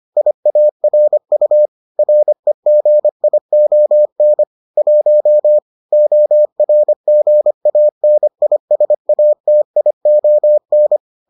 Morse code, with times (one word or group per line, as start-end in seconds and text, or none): 0.2-1.7 IARU
2.0-4.4 REGION
4.8-5.6 1
5.9-11.0 ORGANISATION